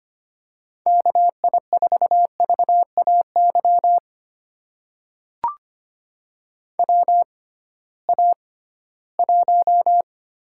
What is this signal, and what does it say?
Morse code, heard as a keyed tone, about 25 wpm, spaced as for 16 wpm, 700 Hz. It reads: KI4VAY WA1
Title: USA Call Signs - 6 Characters - 6x Character Spacing - 25wpm